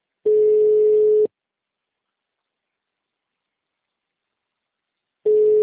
Ringback tone (Europe, 425 Hz).